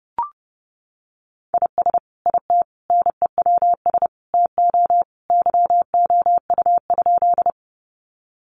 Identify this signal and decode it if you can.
Morse code, heard as a keyed tone, about 28 wpm, 700 Hz. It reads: IS IT NEWS TO YOU?
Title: Sentences from Top 800 Words 30wpm